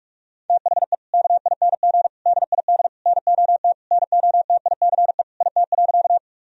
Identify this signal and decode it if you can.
Morse code, heard as a keyed tone, about 46 wpm, 700 Hz. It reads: THE KING DID NOT NOTICE IT.
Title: Sentences from Top 1000 Words - Review Part 2 - 45wpm